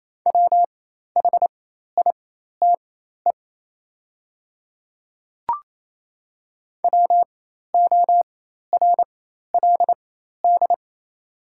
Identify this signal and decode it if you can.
Morse code, heard as a keyed tone, about 28 wpm, spaced as for 13 wpm, 700 Hz. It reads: WHITE WORLD
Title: 4x Character Spacing - Top 300 Words - 28wpm